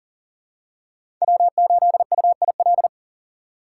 Morse code, heard as a keyed tone, about 40 wpm, 700 Hz. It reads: W8UIL
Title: K1USN SST QSO Practice - 40wpm